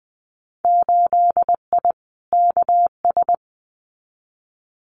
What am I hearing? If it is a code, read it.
Morse code, heard as a keyed tone, about 20 wpm, 700 Hz. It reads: 8I KS
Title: Winter Field Day Exchanges Encore 20wpm